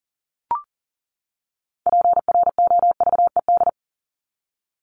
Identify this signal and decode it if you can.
Morse code, heard as a keyed tone, about 40 wpm, 700 Hz. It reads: PROVED
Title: ED Suffix - Top 100 Words with Base Words 40wpm